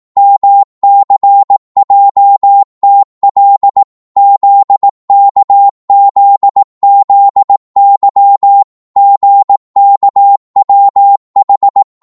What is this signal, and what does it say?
Morse code, heard as a keyed tone, about 18 wpm, 800 Hz. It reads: MCJTL ZKZZY GKWH